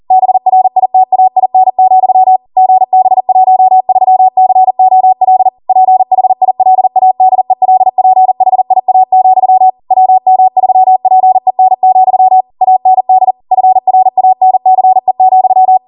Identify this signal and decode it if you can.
Morse code, heard as a keyed tone, about 40 wpm, 750 Hz. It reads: BRITAIN, GB13COL PHILADELPHIA, WM3PEN, AND FRANCE,